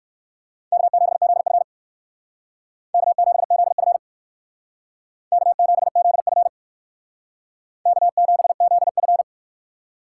Morse code, heard as a keyed tone, about 58 wpm, 700 Hz. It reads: K7ZF K7ZF K7ZF K7ZF